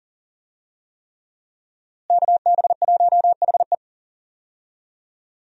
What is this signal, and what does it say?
Morse code, heard as a keyed tone, about 40 wpm, 700 Hz. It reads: KB1HE
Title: USA Call Signs - 5 Characters - Repeated 3x - 40wpm